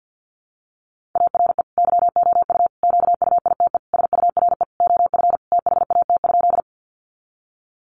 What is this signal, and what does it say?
Morse code, heard as a keyed tone, about 50 wpm, 700 Hz. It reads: ARE YOU QUITE SURE OF THAT?
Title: Sentences from Top 700 Words 50wpm